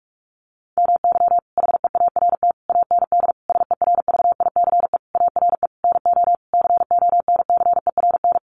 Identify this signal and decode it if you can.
Morse code, heard as a keyed tone, about 44 wpm, 700 Hz. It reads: MY HEART AND SERVICE ARE NO CONCERN
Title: Sentences from Top 1200 Words 45wpm